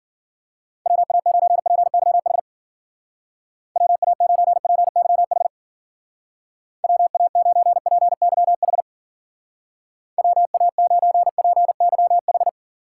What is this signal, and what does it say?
Morse code, heard as a keyed tone, about 51 wpm, 700 Hz. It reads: WA9PYH WA9PYH WA9PYH WA9PYH